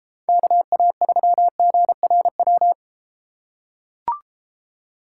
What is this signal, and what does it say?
Morse code, heard as a keyed tone, about 32 wpm, 700 Hz. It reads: KA3GRW